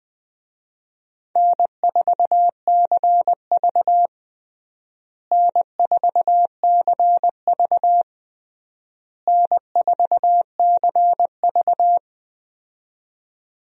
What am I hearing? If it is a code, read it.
Morse code, heard as a keyed tone, about 20 wpm, 700 Hz. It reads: N4CV N4CV N4CV
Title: USA Call Signs - 4 Characters - Repeated 3x - 20wpm